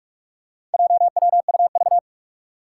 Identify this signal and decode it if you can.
Morse code, heard as a keyed tone, about 43 wpm, 700 Hz. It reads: JWUV
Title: Four Challenging Letter-Number Combinations 45wpm